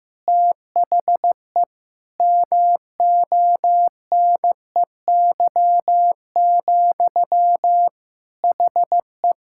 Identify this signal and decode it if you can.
Morse code, heard as a keyed tone, about 15 wpm, 700 Hz. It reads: THE MONEY, HE
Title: Sentences from Top 1000 Words - Review Part 6 - 15wpm